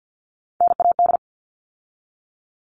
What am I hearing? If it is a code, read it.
Morse code, heard as a keyed tone, about 50 wpm, 700 Hz. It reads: NAD
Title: All Cut Numbers - 3-Digit 50wpm